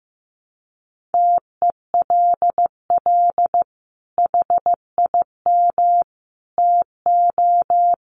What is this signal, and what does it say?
Morse code, heard as a keyed tone, about 15 wpm, 700 Hz. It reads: TELL HIM TO
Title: Sentences from Top 700 Words 15wpm